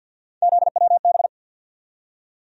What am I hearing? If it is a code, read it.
Morse code, heard as a keyed tone, about 53 wpm, 700 Hz. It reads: ZWB
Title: USA Call Sign Postfixes - 3 Letters 50wpm